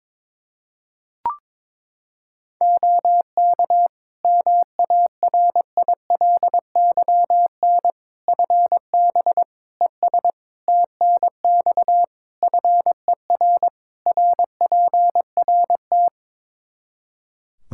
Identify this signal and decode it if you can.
Morse code, heard as a keyed tone, about 22 wpm, 700 Hz. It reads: OK MARILYN FB ES TNX FER RPRT
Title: Practice QSO - Exchange 2 - Info - 22wpm